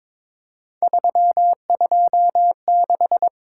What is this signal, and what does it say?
Morse code, heard as a keyed tone, about 22 wpm, 700 Hz. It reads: 326
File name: practice / 22wpm / Three Letter-Number Combinations - 22wpm